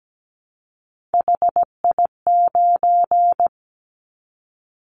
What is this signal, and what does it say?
Morse code, heard as a keyed tone, about 17 wpm, 700 Hz. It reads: HI9
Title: Callsign Prefixes for Top 100 DXCC - 17wpm